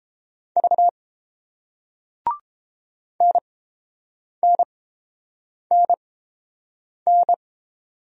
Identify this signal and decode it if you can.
Morse code, heard as a keyed tone, about 31 wpm, spaced as for 7 wpm, 700 Hz. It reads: V NNNN